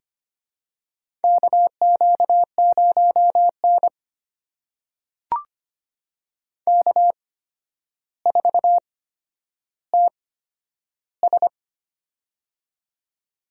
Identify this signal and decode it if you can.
Morse code, heard as a keyed tone, about 25 wpm, spaced as for 11 wpm, 700 Hz. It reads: KQ0N K4TS
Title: USA Call Signs - 4 Characters - 8x Character Spacing - 25wpm